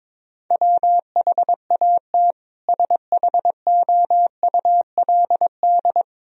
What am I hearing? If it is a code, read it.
Morse code, heard as a keyed tone, about 22 wpm, 700 Hz. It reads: WHAT SHOULD
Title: Sentences from Top 1000 Words - Review Part 4 - 22wpm